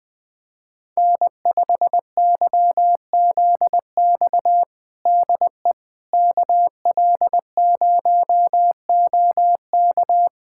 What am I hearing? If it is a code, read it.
Morse code, heard as a keyed tone, about 20 wpm, 700 Hz. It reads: N5YZX DE KL0OK
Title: USA Callsign DE Callsign - 5 Characters - 20wpm